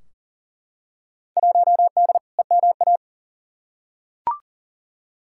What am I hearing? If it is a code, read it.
Morse code, heard as a keyed tone, about 38 wpm, 700 Hz. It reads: 1D EMA
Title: Field Day Exchanges - 40wpm